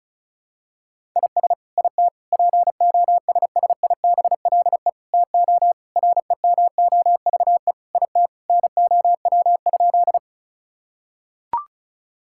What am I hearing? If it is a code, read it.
Morse code, heard as a keyed tone, about 36 wpm, 700 Hz. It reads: IS IT POSSIBLE TO REMOVE IT NOW?
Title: Sentences from Top 1000 Words - Review Part 5 - 35wpm